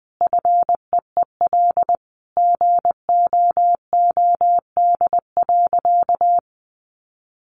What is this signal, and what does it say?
Morse code, heard as a keyed tone, about 20 wpm, 700 Hz. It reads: FEEL GOOD.
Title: Sentences from Top 600 Words 20wpm